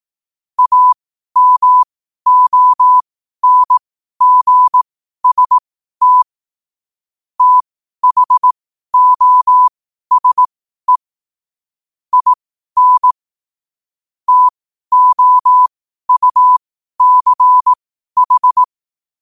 Morse code, heard as a keyed tone, about 18 wpm, spaced as for 12 wpm, 1000 Hz. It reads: AMONGST THOSE IN TOUCH
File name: qotdfiles / 12wpm